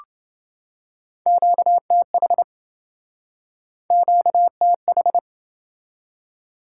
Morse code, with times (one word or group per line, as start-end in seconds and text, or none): 1.3-2.4 QTH
3.9-5.2 QTH